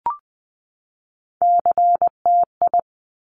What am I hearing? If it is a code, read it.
Morse code, heard as a keyed tone, about 19 wpm, 700 Hz. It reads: CTI
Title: Top Three Letter Combinations in Words 20wpm